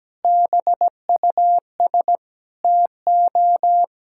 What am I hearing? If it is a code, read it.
Morse code, heard as a keyed tone, about 17 wpm, 700 Hz. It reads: BUS TO
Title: Sentences from Top 1000 Words - Review Part 6 - 17wpm